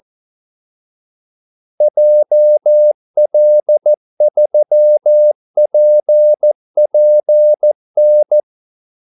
Morse code, heard as a keyed tone, about 14 wpm, 600 Hz. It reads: JL3PPN
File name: Callsigns-14wpm